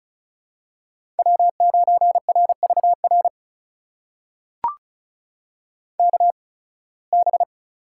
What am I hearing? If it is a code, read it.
Morse code, heard as a keyed tone, about 36 wpm, spaced as for 23 wpm, 700 Hz. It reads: W9RVR KB
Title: USA Call Signs - 5 Characters - 8x Character Spacing - 35wpm